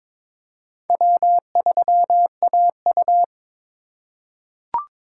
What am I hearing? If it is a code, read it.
Morse code, heard as a keyed tone, about 22 wpm, 700 Hz. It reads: W3AU
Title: USA Call Signs - 2x Character Spacing - 22wpm